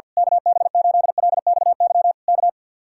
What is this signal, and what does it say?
Morse code, heard as a keyed tone, about 48 wpm, 700 Hz. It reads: KB8LXY K
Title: Speed-Racing Common QSO Phrases Encore - 50wpm